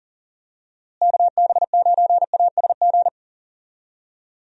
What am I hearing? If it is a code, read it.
Morse code, heard as a keyed tone, about 39 wpm, 700 Hz. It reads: KB9ASG
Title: USA Call Signs - 6 Characters - Repeated 3x - 40wpm